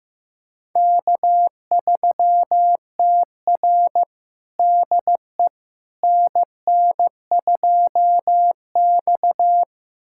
Morse code, heard as a keyed tone, about 16 wpm, 700 Hz. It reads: K3TR DE NN2X
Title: USA Callsign DE Callsign - 4 Characters - 15wpm